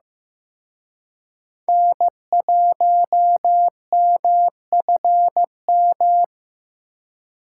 Morse code, heard as a keyed tone, about 18 wpm, 700 Hz. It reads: N1MFM
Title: Speed Racing - USA 1x3 Call Signs 15wpm